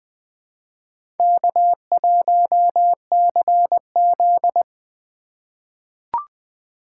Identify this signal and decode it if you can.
Morse code, heard as a keyed tone, about 20 wpm, 700 Hz. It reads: K1CZ
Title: USA Call Signs - 4 Characters - 4x Character Spacing - 20wpm